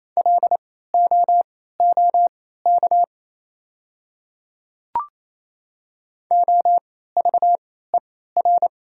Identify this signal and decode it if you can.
Morse code, heard as a keyed tone, about 27 wpm, spaced as for 16 wpm, 700 Hz. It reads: LOOK OVER
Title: 3x Character Spacing - Top 100 Words - 28wpm